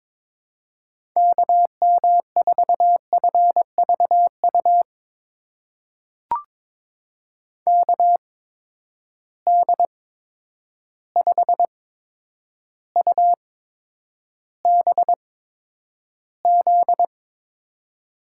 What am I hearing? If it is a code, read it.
Morse code, heard as a keyed tone, about 22 wpm, spaced as for 9 wpm, 700 Hz. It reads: KM4FVU KD5UBZ
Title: USA Call Signs - 6 Characters - 8x Character Spacing - 22wpm